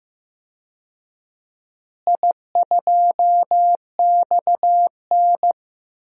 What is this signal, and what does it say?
Morse code, heard as a keyed tone, about 15 wpm, 700 Hz. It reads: I2XN
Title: DXCC-100 Call Signs - 4 Characters - Repeated 2x - 15wpm